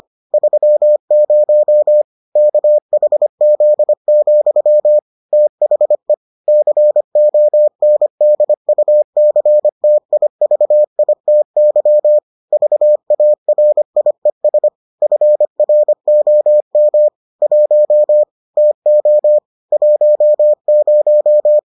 Morse code, heard as a keyed tone, about 25 wpm, 600 Hz. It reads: 30 KHZ, THE CONDUCTIVITY VARIES FROM 1 TO 10